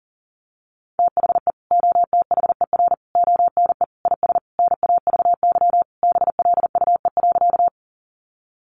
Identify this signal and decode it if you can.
Morse code, heard as a keyed tone, about 40 wpm, 700 Hz. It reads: THE OTHER ONE IS NAVY BLUE.
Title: Additional Easy Phrases 40wpm